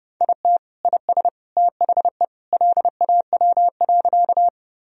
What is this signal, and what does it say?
Morse code, heard as a keyed tone, about 30 wpm, 700 Hz. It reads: IT IS THE LAW.
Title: Sentences from Top 1000 Words - Review Part 3 - 30wpm